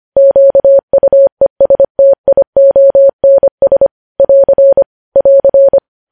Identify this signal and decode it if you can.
Morse code, heard as a keyed tone, about 25 wpm, 550 Hz. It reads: QUESTIONS <AR> <AR>
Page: news headlines in morse code at 25 wpm